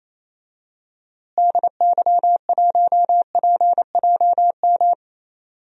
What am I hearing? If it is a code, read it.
Morse code, heard as a keyed tone, about 28 wpm, 700 Hz. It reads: DY1PJM
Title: DXCC-100 Call Signs - 6 Characters - Repeated 3x - 28wpm